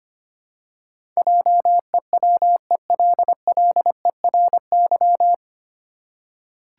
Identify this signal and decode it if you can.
Morse code, heard as a keyed tone, about 25 wpm, 700 Hz. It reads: JEWELLERY
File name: practice / 25wpm / 9 Letter Words - First 3 Letters Given - 25wpm